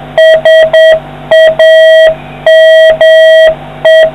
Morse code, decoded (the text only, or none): SAME